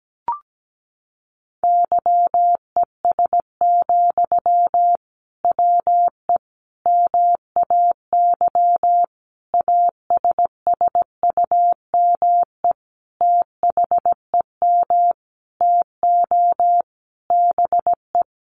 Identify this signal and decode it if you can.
Morse code, heard as a keyed tone, about 17 wpm, 700 Hz. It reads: YES, WE MAY ASSUME THEM TO BE
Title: Sentences from Top 1500 Words 17wpm